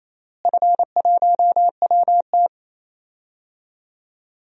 Morse code, heard as a keyed tone, about 28 wpm, 700 Hz. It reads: F1WT
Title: DXCC-100 Call Signs - 4 Characters - Repeated 3x - 28wpm